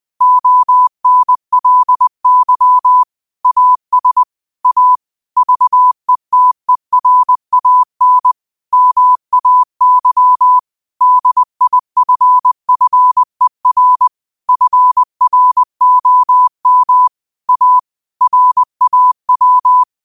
Morse code, heard as a keyed tone, about 20 wpm, 1000 Hz. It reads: ONLY AS A VETERAN MAY DIFFER FROM A RAW
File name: qotdfiles / 20wpm